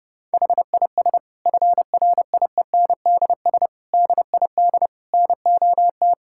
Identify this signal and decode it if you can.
Morse code, heard as a keyed tone, about 32 wpm, 700 Hz. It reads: HIS FRIENDS DID NOT
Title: Sentences from Top 1600 Words 30wpm